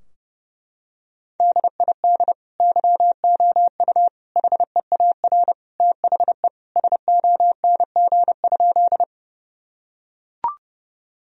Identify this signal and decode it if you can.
Morse code, heard as a keyed tone, about 30 wpm, 700 Hz. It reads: DID YOU HEAR THE SONG?
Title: Sentences from Top 1000 Words - Review Part 2 - 30wpm